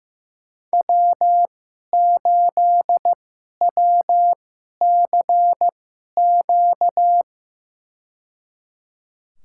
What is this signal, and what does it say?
Morse code, heard as a keyed tone, about 15 wpm, spaced as for 11 wpm, 700 Hz. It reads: W8WCQ
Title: USA Call Signs - 5 Characters - 2x Character Spacing - 15wpm